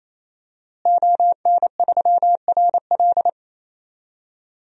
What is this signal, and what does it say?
Morse code, heard as a keyed tone, about 28 wpm, 700 Hz. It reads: ON3RL
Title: DXCC-100 Call Signs - 5 Characters - Repeated 3x - 28wpm